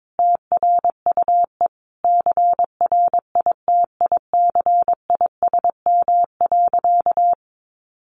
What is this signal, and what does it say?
Morse code, heard as a keyed tone, about 22 wpm, 700 Hz. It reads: TRUE CRITICISM.
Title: Sentences from Top 1300 Words 22wpm